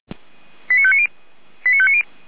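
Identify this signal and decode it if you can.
Five-tone selective call (ZVEI), heard as quick runs of five tones, tones of about 70 ms. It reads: 89590 89590
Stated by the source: Five-tone ZVEI selective call received by SDR.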